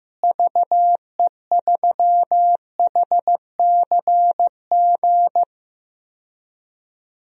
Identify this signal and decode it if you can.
Morse code, heard as a keyed tone, about 15 wpm, 700 Hz. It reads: VE3HCG
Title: DXCC-100 Call Signs - 6 Characters - Repeated 3x - 15wpm